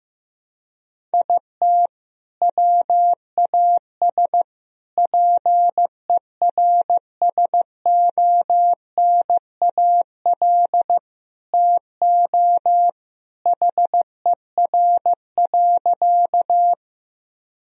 Morse code, heard as a keyed tone, about 15 wpm, 700 Hz. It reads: IT WAS PERSONAL TO HER.
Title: Sentences from Top 1000 Words - Review Part 4 - 15wpm